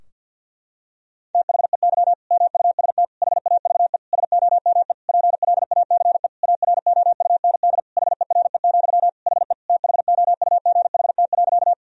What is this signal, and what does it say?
Morse code, heard as a keyed tone, about 51 wpm, 700 Hz. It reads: THEY MUST HAVE SOME PLACE AROUND HERE, HE THOUGHT.
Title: Sentences from Top 1000 Words - Review Part 3 - 50wpm